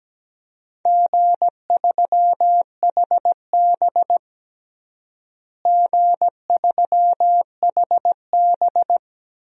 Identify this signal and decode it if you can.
Morse code, heard as a keyed tone, about 17 wpm, 700 Hz. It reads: G3HB G3HB
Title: DXCC-100 Call Signs - 4 Characters - Repeated 2x - 17wpm